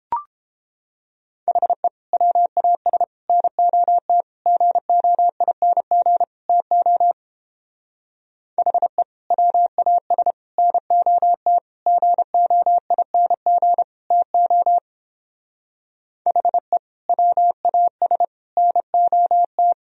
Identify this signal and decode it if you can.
Morse code, heard as a keyed tone, about 29 wpm, 700 Hz. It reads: HE WAS NOT GOING TO HE WAS NOT GOING TO HE WAS NOT